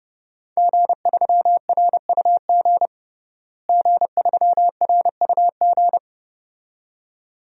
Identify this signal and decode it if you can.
Morse code, heard as a keyed tone, about 30 wpm, 700 Hz. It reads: G3RUG G3RUG
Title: DXCC-100 Call Signs - 5 Characters - Repeated 2x - 30wpm